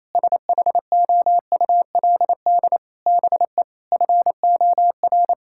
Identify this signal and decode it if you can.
Morse code, heard as a keyed tone, about 28 wpm, 700 Hz. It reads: SHOULD BE FOR
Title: Sentences from Top 1000 Words - Review Part 6 - 28wpm